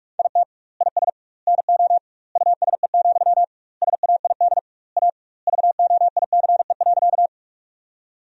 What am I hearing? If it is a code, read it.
Morse code, heard as a keyed tone, about 45 wpm, 700 Hz. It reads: IT IS NO USE, SAID A VOICE.
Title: 2x Word-Spacing - Sentences from Top 500 Words - 45wpm